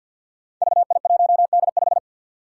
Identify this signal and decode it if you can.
Morse code, heard as a keyed tone, about 47 wpm, 700 Hz. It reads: VI1D5
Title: Five Challenging Letter-Number Combinations 50wpm